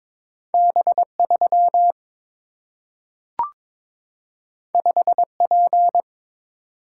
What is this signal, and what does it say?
Morse code, heard as a keyed tone, about 22 wpm, 700 Hz. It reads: B 3 5 P
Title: Single-Letter Single-Number 22wpm